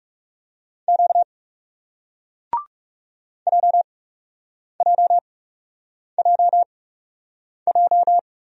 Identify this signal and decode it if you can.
Morse code, heard as a keyed tone, about 39 wpm, spaced as for 8 wpm, 700 Hz. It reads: Q JJJJ